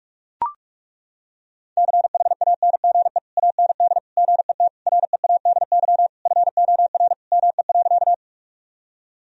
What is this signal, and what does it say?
Morse code, heard as a keyed tone, about 41 wpm, 700 Hz. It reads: CHANGE AND GET READY FOR ME.
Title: Sentences from Top 1000 Words - Review Part 1 - 45wpm